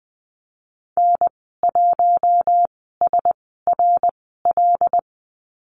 Morse code, heard as a keyed tone, about 19 wpm, spaced as for 14 wpm, 700 Hz. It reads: N1SRL
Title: 082 Lesson 14g - Number 2 - Call Signs 20wpm